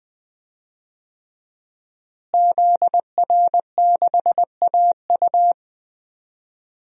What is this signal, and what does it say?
Morse code, heard as a keyed tone, about 20 wpm, 700 Hz. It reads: ZR6AU
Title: DXCC-100 Call Signs - 5 Characters - Repeated 3x - 20wpm